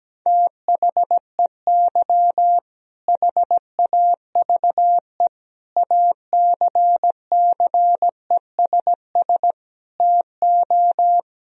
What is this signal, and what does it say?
Morse code, heard as a keyed tone, about 17 wpm, 700 Hz. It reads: THEY HAVE ACCESS TO
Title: Sentences from Top 1000 Words - Review Part 4 - 17wpm